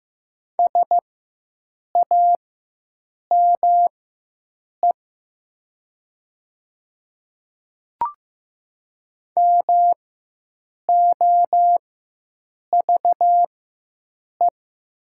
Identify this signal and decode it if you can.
Morse code, heard as a keyed tone, about 15 wpm, spaced as for 7 wpm, 700 Hz. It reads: SAME MOVE
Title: 4x Character Spacing - Top 200 Words - 15wpm